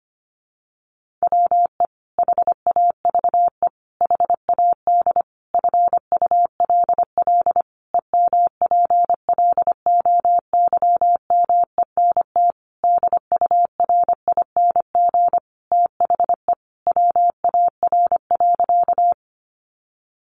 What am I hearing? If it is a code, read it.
Morse code, heard as a keyed tone, about 25 wpm, 700 Hz. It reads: WE HAVE HAD FULL EMPLOYMENT DURING THE WAR.
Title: Sentences from Top 1500 Words 25wpm